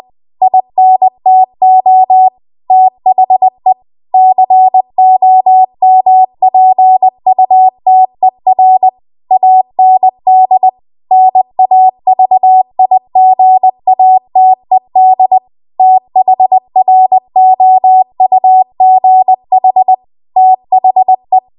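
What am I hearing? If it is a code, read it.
Morse code, heard as a keyed tone, about 20 wpm, 750 Hz. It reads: INTO THE COMPUTER AND NAVIGATED THROUGH THE
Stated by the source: Morse code practice at 20 WPM.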